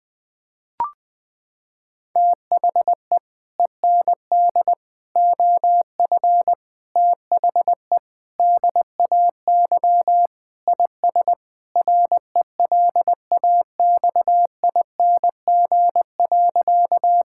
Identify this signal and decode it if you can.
Morse code, heard as a keyed tone, about 20 wpm, 700 Hz. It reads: THE END OF THE DAY IS RELAXING.
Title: Sentences with 3-Word Phrases - 20wpm